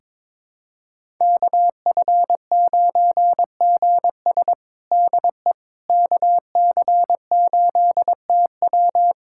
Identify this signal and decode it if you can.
Morse code, heard as a keyed tone, about 21 wpm, 700 Hz. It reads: KF9GS DE KC8TW
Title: USA Callsign DE Callsign - 5 Characters - 22wpm